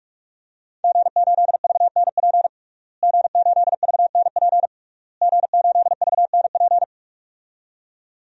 Morse code, heard as a keyed tone, about 46 wpm, 700 Hz. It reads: G8VNP G8VNP G8VNP
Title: DXCC-100 Call Signs - 5 Characters - Repeated 3x - 45wpm